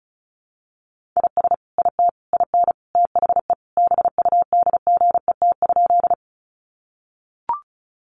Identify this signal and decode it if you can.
Morse code, heard as a keyed tone, about 34 wpm, 700 Hz. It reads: IS IT IN THE BUDGET?
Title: Sentences from Top 1200 Words 35wpm